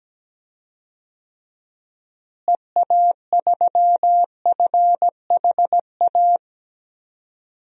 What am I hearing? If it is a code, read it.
Morse code, heard as a keyed tone, about 17 wpm, 700 Hz. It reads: EA3FHA
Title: DXCC-100 Call Signs - 6 Characters - Repeated 3x - 17wpm